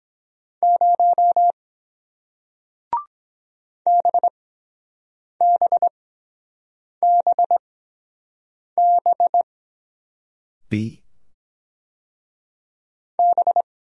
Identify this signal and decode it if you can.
Morse code, heard as a keyed tone, about 22 wpm, spaced as for 7 wpm, 700 Hz. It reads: 0 BBBB B